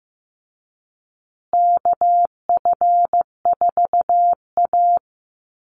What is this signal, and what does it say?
Morse code, heard as a keyed tone, about 15 wpm, 700 Hz. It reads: KF4A
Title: Call Signs Repeated 2x - 15wpm